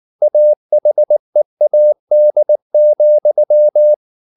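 Morse code, heard as a keyed tone, about 19 wpm, 600 Hz. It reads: AHEAD,
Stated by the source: Morse practice at 18 wpm, English text, from CW med Gnister.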